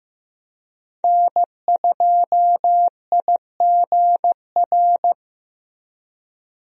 Morse code, heard as a keyed tone, about 15 wpm, 700 Hz. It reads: N2IGR
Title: USA Call Signs - 5 Characters - Repeated 2x - 15wpm